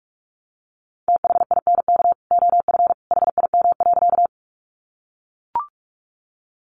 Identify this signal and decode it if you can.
Morse code, heard as a keyed tone, about 44 wpm, 700 Hz. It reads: THINK OF HIM.
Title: Sentences from Top 200 Words 45wpm